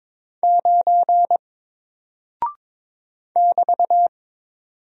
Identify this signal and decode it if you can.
Morse code, heard as a keyed tone, about 22 wpm, 700 Hz. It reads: 9 <BT>
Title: Single Character (Letter:Number:Punctuation:Prosign) 22wpm